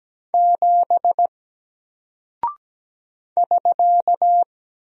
Morse code, heard as a keyed tone, about 17 wpm, 700 Hz. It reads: 7 <SK>
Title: Single Character (Letter:Number:Punctuation:Prosign) Alt 17wpm